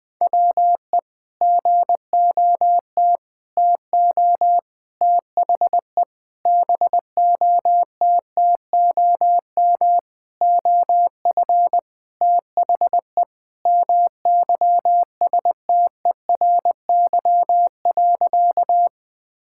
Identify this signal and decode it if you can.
Morse code, heard as a keyed tone, about 20 wpm, 700 Hz. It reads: WE GOT TO THE BOTTOM OF THE MYSTERY.
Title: Sentences with 5-Word Phrases - 20wpm